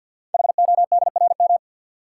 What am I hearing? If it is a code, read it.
Morse code, heard as a keyed tone, about 52 wpm, 700 Hz. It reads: HODRM
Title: Code Groups - 5 Letters - 50wpm